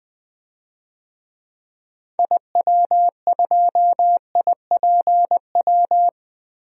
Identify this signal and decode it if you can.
Morse code, heard as a keyed tone, about 21 wpm, 700 Hz. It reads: IW2IPW
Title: DXCC-100 Call Signs - 6 Characters - 20wpm